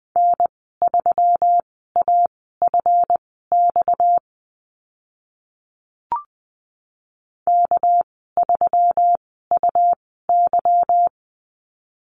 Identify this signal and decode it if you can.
Morse code, heard as a keyed tone, about 20 wpm, spaced as for 14 wpm, 700 Hz. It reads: N3AFX K3UY
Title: 245 Lesson 39g - Letter z - Call Signs 20wpm